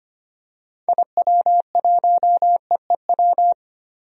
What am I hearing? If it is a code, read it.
Morse code, heard as a keyed tone, about 25 wpm, 700 Hz. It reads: IW1EEW
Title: DXCC-100 Call Signs - 6 Characters - 25wpm